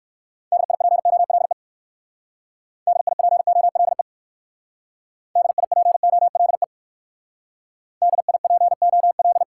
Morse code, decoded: DIPOLE DIPOLE DIPOLE DIPOL

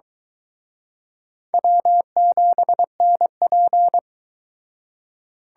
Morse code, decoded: W7NP

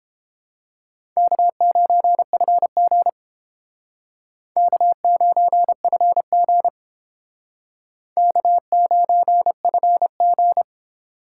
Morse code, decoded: K9FG K9FG K9FG

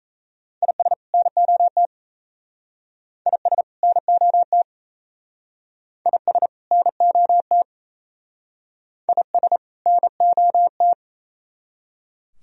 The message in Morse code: IS NOT IS NOT IS NOT IS NOT

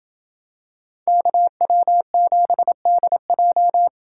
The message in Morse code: KW7DJ

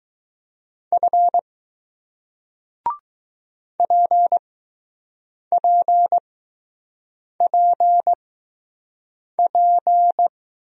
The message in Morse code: F PPPP

